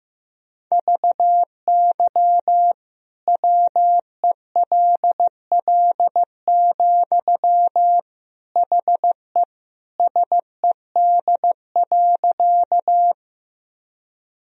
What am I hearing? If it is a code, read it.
Morse code, heard as a keyed tone, about 15 wpm, 700 Hz. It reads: VY WELL, HE SED.
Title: Sentences from Top 200 Words and Abbreviations 15wpm